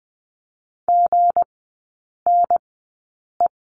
Morse code, heard as a keyed tone, about 20 wpm, spaced as for 8 wpm, 700 Hz. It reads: GNE
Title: 184 Lesson 30e - Letter b - Three Characters 20wpm